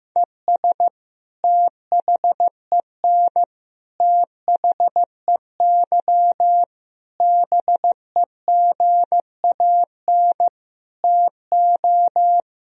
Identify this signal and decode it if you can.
Morse code, heard as a keyed tone, about 16 wpm, 700 Hz. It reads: ES THEN THEY BEGAN TO
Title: Sentences from Top 300 Words and Abbreviations 15wpm